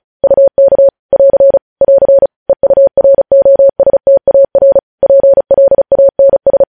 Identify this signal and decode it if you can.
Morse code, heard as a keyed tone, about 36 wpm, 550 Hz. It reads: UK <AR> <AR> EUROSTAR PLANS
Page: news headlines in morse code at 35 wpm